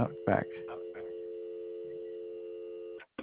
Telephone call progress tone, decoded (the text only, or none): dial tone